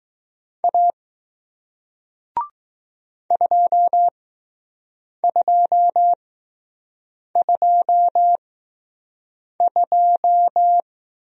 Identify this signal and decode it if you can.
Morse code, heard as a keyed tone, about 19 wpm, spaced as for 6 wpm, 700 Hz. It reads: A 2222